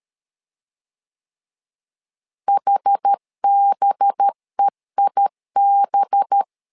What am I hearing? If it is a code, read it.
Morse code, heard as a keyed tone, about 13 wpm, 800 Hz. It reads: HBEIB
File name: LEZIONI / 5 LBVC / 5 LB 12wpm 1906